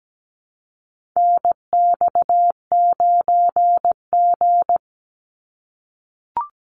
Morse code, heard as a keyed tone, about 17 wpm, 700 Hz. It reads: NX9G
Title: Call Signs - 17wpm